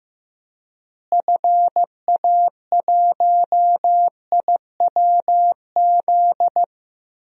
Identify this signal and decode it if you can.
Morse code, heard as a keyed tone, about 15 wpm, 700 Hz. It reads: FA1IWZ
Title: DXCC-100 Call Signs - 6 Characters - Repeated 3x - 15wpm